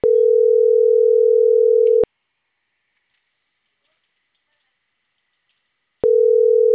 Ringback tone (North America), 440 + 480 Hz.